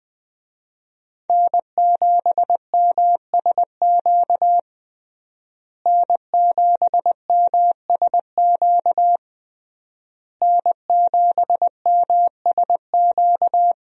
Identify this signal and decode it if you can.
Morse code, heard as a keyed tone, about 20 wpm, 700 Hz. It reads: N7MSQ N7MSQ N7MSQ
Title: USA Call Signs - 5 Characters - Repeated 3x - 20wpm